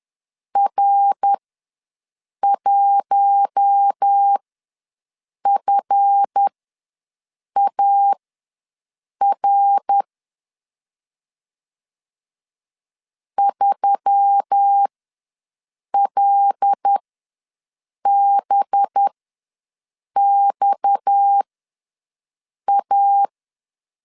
Morse code, heard as a keyed tone, about 10 wpm, spaced as for 6 wpm, 800 Hz. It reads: R1FAR 3LBXA
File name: LEZIONI / 7 GF9Z / 7 GF ws 10wpm 3918